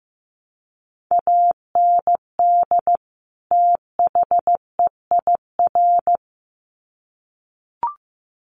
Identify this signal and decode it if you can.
Morse code, heard as a keyed tone, about 14 wpm, 700 Hz. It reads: AND THEIR
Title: Sets of 2 Words 15wpm